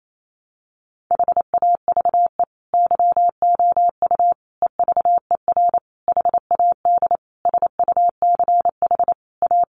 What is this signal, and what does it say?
Morse code, heard as a keyed tone, about 28 wpm, 700 Hz. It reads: HAVE YOU EVER HAD SUCH A
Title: Sentences from Top 1200 Words 28wpm